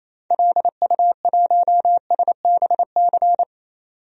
Morse code, heard as a keyed tone, about 28 wpm, 700 Hz. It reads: LU1SBC
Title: DXCC-100 Call Signs - 6 Characters - 28wpm